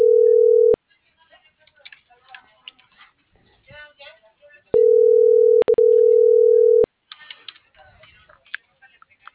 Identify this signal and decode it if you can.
Ringback tone (North America), 440 + 480 Hz.